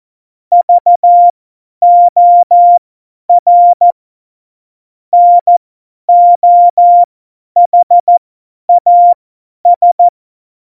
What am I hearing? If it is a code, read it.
Morse code, heard as a keyed tone, about 14 wpm, spaced as for 10 wpm, 700 Hz. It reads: VOR NOHAS